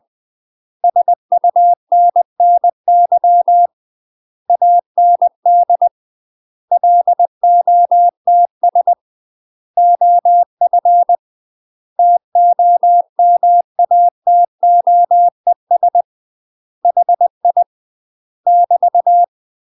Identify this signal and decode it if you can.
Morse code, heard as a keyed tone, about 20 wpm, 700 Hz. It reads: SUNNY AND LOTS OF TOMATOES HI <BT>
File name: UK MASTER QSOs TRAINING / 20WPM / UK MASTER QSOs 0034